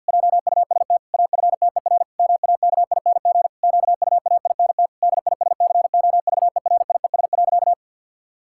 Morse code, heard as a keyed tone, about 50 wpm, 700 Hz. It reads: JUST AFTER MAKING QUAINT DISCOVERIES.
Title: Pangram Challenge v2 - 50wpm